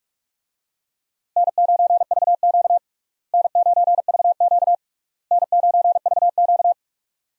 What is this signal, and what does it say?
Morse code, heard as a keyed tone, about 46 wpm, 700 Hz. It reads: N9VQ N9VQ N9VQ
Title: USA Call Signs - 4 Characters - Repeated 3x - 45wpm